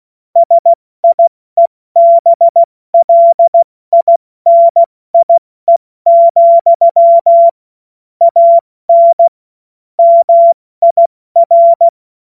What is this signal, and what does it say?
Morse code, heard as a keyed tone, about 16 wpm, spaced as for 14 wpm, 680 Hz. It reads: SIEBLINIE, AN MIR
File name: Morsekurs/Witze/Witz 12 Dezember 2023 70 F80